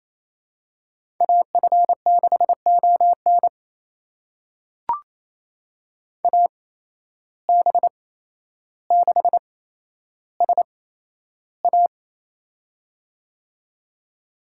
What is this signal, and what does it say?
Morse code, heard as a keyed tone, about 28 wpm, spaced as for 12 wpm, 700 Hz. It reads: AF6ON AB6SA
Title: USA Call Signs - 5 Characters - 8x Character Spacing - 28wpm